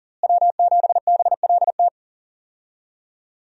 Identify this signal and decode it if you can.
Morse code, heard as a keyed tone, about 40 wpm, 700 Hz. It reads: W7BLT
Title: USA Call Signs - 5 Characters - Repeated 2x - 40wpm